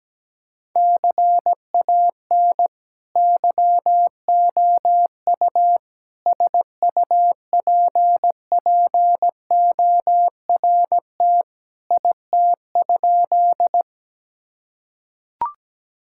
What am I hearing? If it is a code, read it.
Morse code, heard as a keyed tone, about 17 wpm, 700 Hz. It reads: CAN YOU SUPPORT IT?
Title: Sentences from Top 1000 Words - Review Part 5 - 17wpm